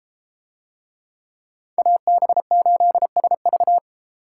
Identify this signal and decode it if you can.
Morse code, heard as a keyed tone, about 32 wpm, 700 Hz. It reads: AB8SV